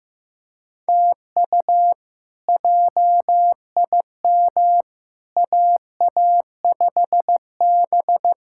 Morse code, heard as a keyed tone, about 15 wpm, 700 Hz. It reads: TU JIM AA5B
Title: K1USN SST QSO Practice - 15wpm